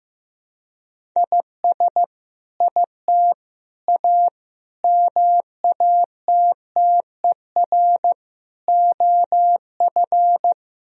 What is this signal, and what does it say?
Morse code, heard as a keyed tone, about 15 wpm, 700 Hz. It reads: IS IT A MATTER OF
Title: Sentences from Top 1000 Words - Review Part 3 - 15wpm